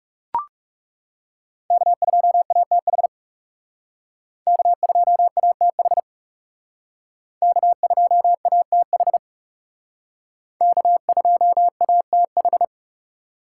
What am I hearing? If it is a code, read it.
Morse code, heard as a keyed tone, about 38 wpm, 700 Hz. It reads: K2ATH K2ATH K2ATH K2ATH